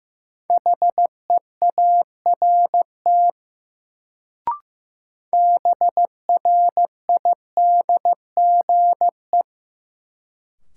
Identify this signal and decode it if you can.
Morse code, heard as a keyed tone, about 15 wpm, 700 Hz. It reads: HEART BRIDGE
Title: Sentences from Top 1200 Words - New Word Drill - 15wpm